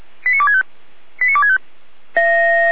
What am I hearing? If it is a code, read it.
Five-tone selective call (ZVEI), heard as quick runs of five tones, tones of about 70 ms. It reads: 89265 89265
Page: /455/SDR/ZVEI/89400/